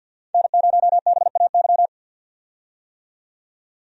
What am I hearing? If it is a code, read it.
Morse code, heard as a keyed tone, about 50 wpm, 700 Hz. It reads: N0BAY
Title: USA Call Signs - 5 Characters - 50wpm